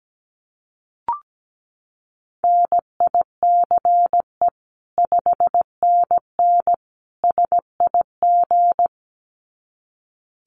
Morse code, heard as a keyed tone, about 17 wpm, 700 Hz. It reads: NICE 5NN SIG
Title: Common QSO Phrases 17wpm